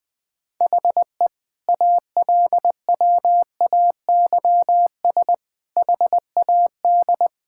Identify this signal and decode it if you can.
Morse code, heard as a keyed tone, about 20 wpm, 700 Hz. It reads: HE ALWAYS HAD
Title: Sentences from Top 1600 Words 20wpm